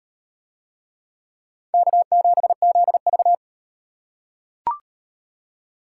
Morse code, heard as a keyed tone, about 35 wpm, 700 Hz. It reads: K7ZV